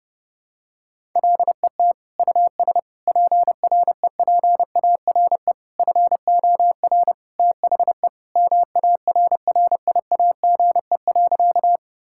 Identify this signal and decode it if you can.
Morse code, heard as a keyed tone, about 32 wpm, 700 Hz. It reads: LET US PREPARE FOR THE MARRIAGE.
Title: Sentences from Top 1000 Words - Review Part 6 - 30wpm